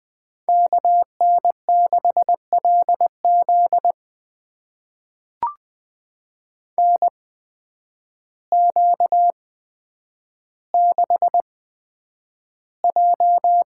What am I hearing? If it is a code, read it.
Morse code, heard as a keyed tone, about 20 wpm, spaced as for 9 wpm, 700 Hz. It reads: KN6LZ NQ6J
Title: USA Call Signs - 8x Character Spacing - 20wpm